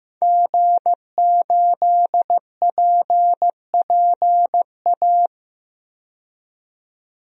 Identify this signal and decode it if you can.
Morse code, heard as a keyed tone, about 15 wpm, 700 Hz. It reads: G8PPA
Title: DXCC-100 Call Signs - 5 Characters - 15wpm